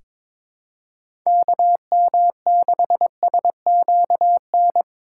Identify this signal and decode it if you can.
Morse code, heard as a keyed tone, about 22 wpm, 700 Hz. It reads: KM6SQN
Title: USA Call Signs - 6 Characters - 2x Character Spacing - 22wpm